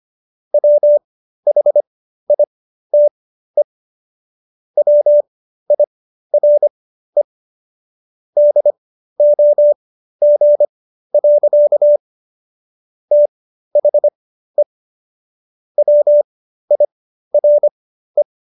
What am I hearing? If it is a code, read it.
Morse code, heard as a keyed tone, about 25 wpm, spaced as for 13 wpm, 590 Hz. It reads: WHITE WIRE DOG. THE WIRE